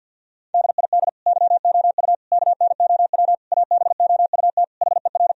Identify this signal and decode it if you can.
Morse code, heard as a keyed tone, about 50 wpm, 700 Hz. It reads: DID YOU KNOW ABOUT HER
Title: Sentences from Top 1000 Words - Review Part 4 - 50wpm